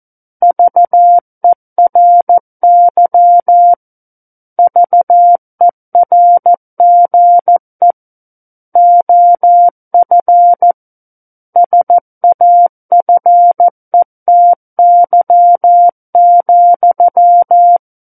Morse code, heard as a keyed tone, about 14 wpm, 700 Hz. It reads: VERY VERGE OF SAFETY,
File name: Book One / 05-Chapter 5 - 14 WPM